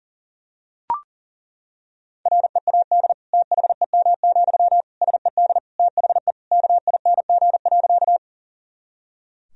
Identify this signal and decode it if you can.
Morse code, heard as a keyed tone, about 40 wpm, 700 Hz. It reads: READ THEM, SED THE KING.
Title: Sentences from Top 400 Words and Abbreviations 40wpm